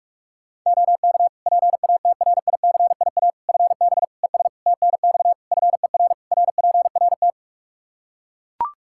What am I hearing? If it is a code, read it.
Morse code, heard as a keyed tone, about 41 wpm, 700 Hz. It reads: OK PATRICIA FB ES TNX FER RPRT
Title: Practice QSO - Exchange 2 - Info - 45wpm